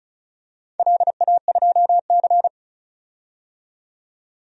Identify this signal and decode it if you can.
Morse code, heard as a keyed tone, about 36 wpm, 700 Hz. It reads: LA2C
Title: DXCC-100 Call Signs - 4 Characters - 35wpm